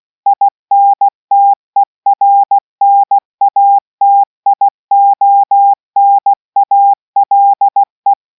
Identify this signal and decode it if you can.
Morse code, heard as a keyed tone, about 16 wpm, 800 Hz. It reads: INTERNATIONALE